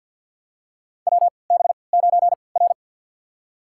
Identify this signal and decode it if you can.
Morse code, heard as a keyed tone, about 49 wpm, spaced as for 28 wpm, 700 Hz. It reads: WB9R
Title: USA Call Signs - 4 Characters - 3x Character Spacing - 50wpm